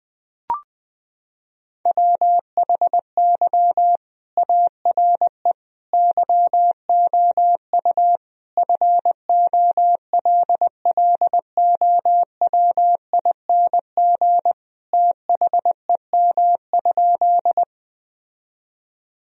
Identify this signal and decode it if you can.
Morse code, heard as a keyed tone, about 20 wpm, 700 Hz. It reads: WHY ARE YOU FOLLOWING THEM?
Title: Sentences from Top 1000 Words - Review Part 4 - 20wpm